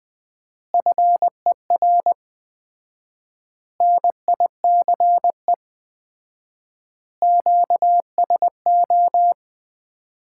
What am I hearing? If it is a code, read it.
Morse code, heard as a keyed tone, about 20 wpm, 700 Hz. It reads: FER NICE QSO
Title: Common QSO Phrases - 4x Word-Spacing - 20wpm